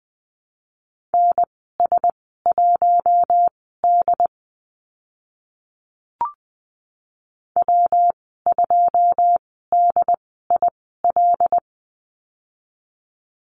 Morse code, heard as a keyed tone, about 20 wpm, spaced as for 14 wpm, 700 Hz. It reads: NS1D W2DIL
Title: 122 Lesson 20g - Number 3 - Call Signs 20wpm